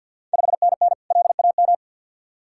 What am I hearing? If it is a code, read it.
Morse code, heard as a keyed tone, about 50 wpm, 700 Hz. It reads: 5NN LAM
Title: Ontario QSO Party Exchange 50wpm